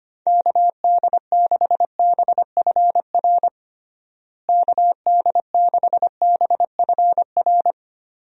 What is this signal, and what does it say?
Morse code, heard as a keyed tone, about 25 wpm, 700 Hz. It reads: KD6BFR KD6BFR
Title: USA Call Signs - 6 Characters - Repeated 2x - 25wpm